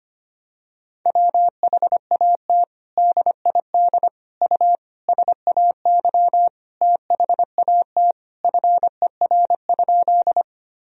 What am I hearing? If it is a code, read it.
Morse code, heard as a keyed tone, about 25 wpm, 700 Hz. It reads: WHAT DID U SAY THAT FER?
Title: Sentences from Top 200 Words and Abbreviations 25wpm